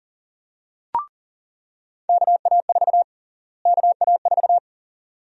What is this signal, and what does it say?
Morse code, heard as a keyed tone, about 40 wpm, 700 Hz. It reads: KA4 KA4
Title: USA Call Sign Prefixes - Repeated 2x - 40wpm